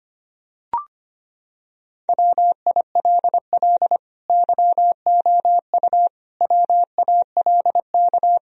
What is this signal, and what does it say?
Morse code, heard as a keyed tone, about 25 wpm, 700 Hz. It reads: WILL YOU WALK
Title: Sentences from Top 1000 Words - Review Part 2 - 25wpm